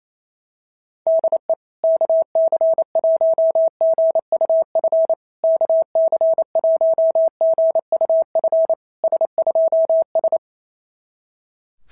Morse code, heard as a keyed tone, about 25 wpm, 650 Hz. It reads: DE KC1GUF KC1GUF S2S
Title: SOTA QSO Practice 28wpm